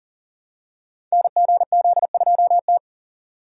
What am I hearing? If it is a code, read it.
Morse code, heard as a keyed tone, about 43 wpm, 700 Hz. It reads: NGZ2T
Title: Code Groups - 5 Letter:Numbers - 40wpm